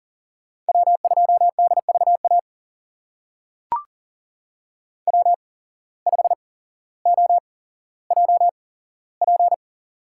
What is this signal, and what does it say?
Morse code, heard as a keyed tone, about 40 wpm, spaced as for 17 wpm, 700 Hz. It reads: W2DVA W5OJP
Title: USA Call Signs - 5 Characters - 8x Character Spacing - 40wpm